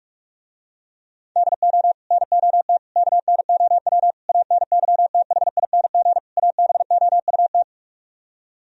Morse code, heard as a keyed tone, about 44 wpm, 700 Hz. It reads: DO NOT KNOW ANYTHING ABOUT